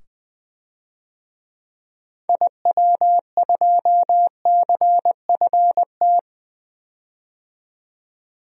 Morse code, heard as a keyed tone, about 20 wpm, 700 Hz. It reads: IW2CFT
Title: DXCC-100 Call Signs - 6 Characters - 20wpm